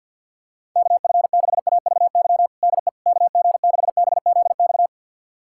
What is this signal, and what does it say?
Morse code, heard as a keyed tone, about 52 wpm, 700 Hz. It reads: KF6AVY DE KG6BZX
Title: USA Callsign DE Callsign - 6 Characters - 50wpm